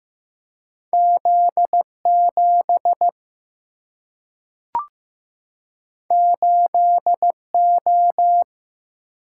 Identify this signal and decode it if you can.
Morse code, heard as a keyed tone, about 15 wpm, 700 Hz. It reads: Z 7 8 O
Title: Single-Letter Single-Number 15wpm